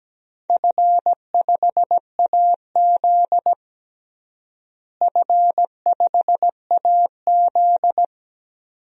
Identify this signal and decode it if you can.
Morse code, heard as a keyed tone, about 17 wpm, 700 Hz. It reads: F5AZ F5AZ
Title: DXCC-100 Call Signs - 4 Characters - Repeated 2x - 17wpm